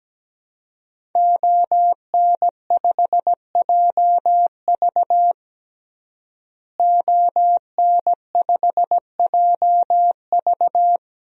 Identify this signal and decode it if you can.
Morse code, heard as a keyed tone, about 17 wpm, 700 Hz. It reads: ON5JV ON5JV
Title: DXCC-100 Call Signs - 5 Characters - Repeated 2x - 17wpm